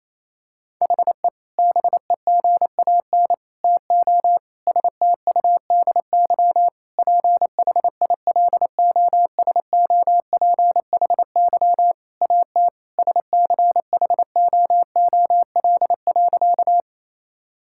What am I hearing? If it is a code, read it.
Morse code, heard as a keyed tone, about 28 wpm, 700 Hz. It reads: HE BEGAN TO STUDY PHILOSOPHY AT SCHOOL.
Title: Sentences from Top 1000 Words - Review Part 6 - 28wpm